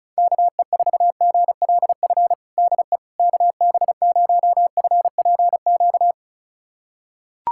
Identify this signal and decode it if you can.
Morse code, heard as a keyed tone, about 36 wpm, 700 Hz. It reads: KE4GLF DE KB0FPQ
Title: USA Callsign DE Callsign 35wpm